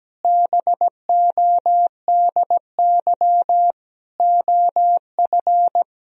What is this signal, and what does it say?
Morse code, heard as a keyed tone, about 17 wpm, 700 Hz. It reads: BODY OF
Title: Sentences from Top 1000 Words - Review Part 4 - 17wpm